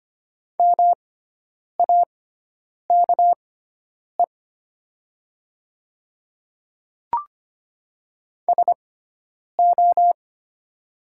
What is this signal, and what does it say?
Morse code, heard as a keyed tone, about 25 wpm, spaced as for 9 wpm, 700 Hz. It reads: MAKE SO
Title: 6x Character Spacing - Top 100 Words - 25wpm